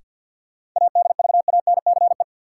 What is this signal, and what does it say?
Morse code, heard as a keyed tone, about 53 wpm, 700 Hz. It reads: ADVANCE
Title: Sentences from Top 1400 Words - New Word Drill Alt - 50wpm